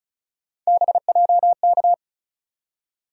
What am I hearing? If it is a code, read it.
Morse code, heard as a keyed tone, about 35 wpm, 700 Hz. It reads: BJK
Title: USA Call Sign Postfixes - 3 Letters 35wpm